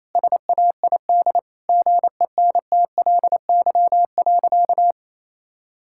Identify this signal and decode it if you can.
Morse code, heard as a keyed tone, about 28 wpm, 700 Hz. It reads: SAID GENTLY.
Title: Sentences from Top 1600 Words 28wpm